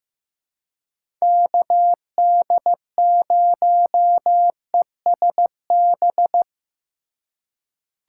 Morse code, heard as a keyed tone, about 15 wpm, 700 Hz. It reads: KD0ESB
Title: USA Call Signs - 2x Character Spacing - 15wpm